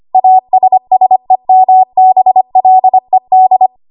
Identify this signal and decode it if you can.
Morse code, heard as a keyed tone, about 25 wpm, 750 Hz. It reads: ASSEMBLED
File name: Archive / 25 WPM / 25WPM